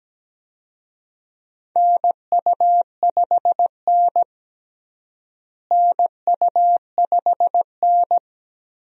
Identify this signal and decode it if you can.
Morse code, heard as a keyed tone, about 17 wpm, 700 Hz. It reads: NU5N NU5N
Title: USA Call Signs - 4 Characters - Repeated 3x - 17wpm